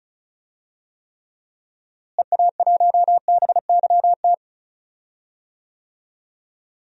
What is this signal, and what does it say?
Morse code, heard as a keyed tone, about 34 wpm, 700 Hz. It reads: EA1BYT
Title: DXCC-100 Call Signs - 6 Characters - Repeated 3x - 35wpm